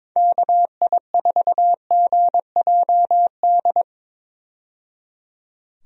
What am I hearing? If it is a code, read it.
Morse code, heard as a keyed tone, about 22 wpm, 700 Hz. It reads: KI4GJD
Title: USA Call Signs - 6 Characters - 22wpm